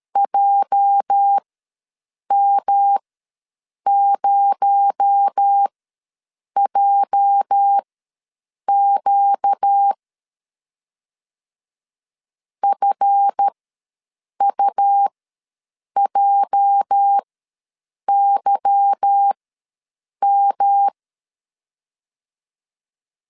Morse code, heard as a keyed tone, about 13 wpm, spaced as for 7 wpm, 800 Hz. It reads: JM0JQ FUJYM
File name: LEZIONI / 8 YJ45 / 8 YJ ws 12wpm 3342